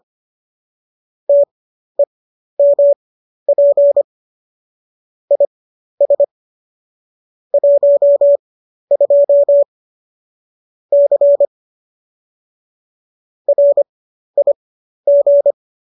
Morse code, heard as a keyed tone, about 25 wpm, spaced as for 12 wpm, 570 Hz. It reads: TEMP IS 12 C RIG
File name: qso009 12